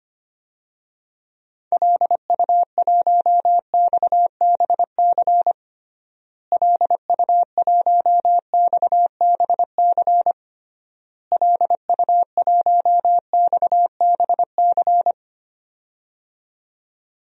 Morse code, heard as a keyed tone, about 25 wpm, 700 Hz. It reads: LU1XBC LU1XBC LU1XBC
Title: DXCC-100 Call Signs - 6 Characters - Repeated 3x - 25wpm